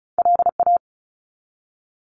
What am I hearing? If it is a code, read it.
Morse code, heard as a keyed tone, about 36 wpm, 700 Hz. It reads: LA
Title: US States 35wpm